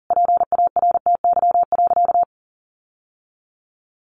Morse code, heard as a keyed tone, about 40 wpm, 700 Hz. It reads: PARTY.
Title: Sentences from Top 1400 Words 40wpm